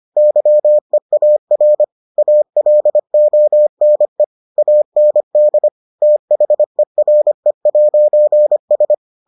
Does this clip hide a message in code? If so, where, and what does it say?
Yes: Morse code from 0.2 to 8.9 s, heard as a keyed tone, about 25 wpm, 600 Hz. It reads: YEAR ALONE AND THERE'S